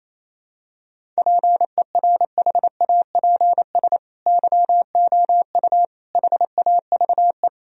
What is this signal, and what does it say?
Morse code, heard as a keyed tone, about 28 wpm, 700 Hz. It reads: PERHAPS YOU HAVE
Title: Sentences from Top 1000 Words - Review Part 1 - 28wpm